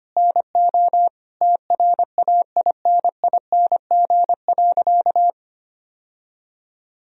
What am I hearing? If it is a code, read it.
Morse code, heard as a keyed tone, about 25 wpm, 700 Hz. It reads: NO TRAINING.
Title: Sentences from Top 1000 Words - Review Part 4 - 25wpm